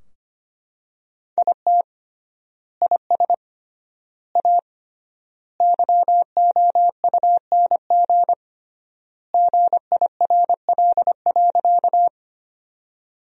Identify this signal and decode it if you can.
Morse code, heard as a keyed tone, about 25 wpm, 700 Hz. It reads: IT IS A YOUNG GIRL.
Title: 3x Word-Spacing - Sentences from Top 400 Words - 25wpm